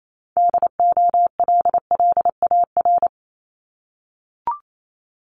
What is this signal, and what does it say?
Morse code, heard as a keyed tone, about 28 wpm, 700 Hz. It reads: DOLLAR
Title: Multi-letter Phonemes 28wpm